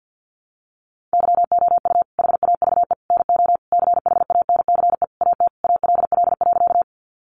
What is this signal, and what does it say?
Morse code, heard as a keyed tone, about 50 wpm, 700 Hz. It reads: YOU HAVE NO CHANCE AT ALL.
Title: Sentences from Top 800 Words 50wpm